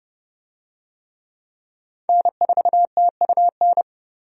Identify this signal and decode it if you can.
Morse code, heard as a keyed tone, about 30 wpm, 700 Hz. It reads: N4TUN
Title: USA Call Signs - 5 Characters - 30wpm